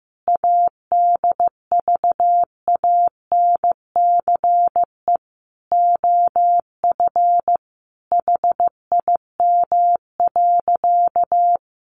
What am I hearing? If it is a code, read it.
Morse code, heard as a keyed tone, about 15 wpm, 700 Hz. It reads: ADVANCE OF HIM.
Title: Sentences from Top 1400 Words 15wpm